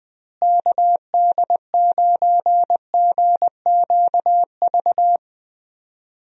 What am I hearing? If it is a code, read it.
Morse code, heard as a keyed tone, about 20 wpm, 700 Hz. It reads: KD9GQV
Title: USA Call Signs - 6 Characters - Repeated 3x - 20wpm